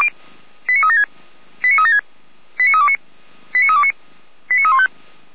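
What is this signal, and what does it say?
Five-tone selective call (ZVEI), heard as quick runs of five tones, tones of about 70 ms. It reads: 89376 89376 89329 89329 89315
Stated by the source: ZVEI five-tone selective call received by SDR.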